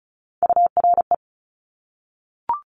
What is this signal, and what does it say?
Morse code, heard as a keyed tone, about 36 wpm, 700 Hz. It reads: URE
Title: Top Three Letter Combinations in Words 35wpm